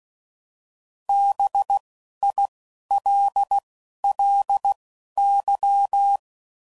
Morse code, heard as a keyed tone, about 16 wpm, spaced as for 11 wpm, 800 Hz. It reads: BILLY